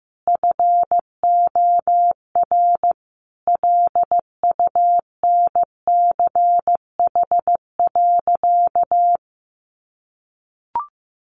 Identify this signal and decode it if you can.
Morse code, heard as a keyed tone, about 15 wpm, 700 Hz. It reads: FOR LUNCH.
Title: Easy Phrases 15wpm